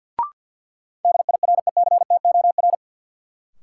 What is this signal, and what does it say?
Morse code, heard as a keyed tone, about 53 wpm, 700 Hz. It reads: DIRECTOR
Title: Sentences from Top 1300 Words - New Word Drill Alt - 50wpm